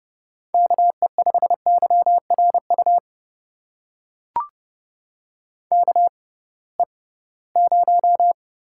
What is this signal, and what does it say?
Morse code, heard as a keyed tone, about 29 wpm, spaced as for 19 wpm, 700 Hz. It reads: KE5YRU KE0
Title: USA Call Signs - 6 Characters - 6x Character Spacing - 30wpm